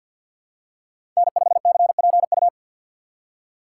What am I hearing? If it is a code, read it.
Morse code, heard as a keyed tone, about 51 wpm, 700 Hz. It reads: N5CPU
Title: USA Call Signs - 5 Characters - 3x Character Spacing - 50wpm